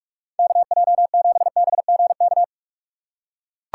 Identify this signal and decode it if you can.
Morse code, heard as a keyed tone, about 43 wpm, 700 Hz. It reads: KJ7BGK
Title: USA Call Signs - 6 Characters - 2x Character Spacing - 45wpm